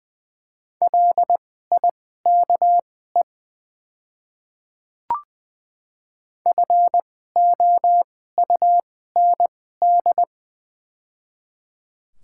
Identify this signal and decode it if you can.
Morse code, heard as a keyed tone, about 20 wpm, spaced as for 15 wpm, 700 Hz. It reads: LIKE FOUND
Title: 2x Character Spacing - Top 300 Words - 20wpm